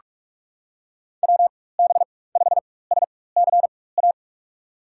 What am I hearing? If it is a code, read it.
Morse code, heard as a keyed tone, about 46 wpm, spaced as for 21 wpm, 700 Hz. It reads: WB5SCA
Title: USA Call Signs - 6 Characters - 4x Character Spacing - 45wpm